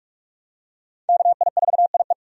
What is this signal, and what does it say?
Morse code, heard as a keyed tone, about 46 wpm, 700 Hz. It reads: KI4IE
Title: USA Call Signs - 5 Characters - 45wpm